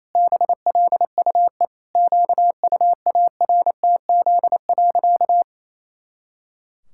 Morse code, heard as a keyed tone, about 28 wpm, 700 Hz. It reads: BLUE QUARTZ.
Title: Pangram Challenge v2 - 28wpm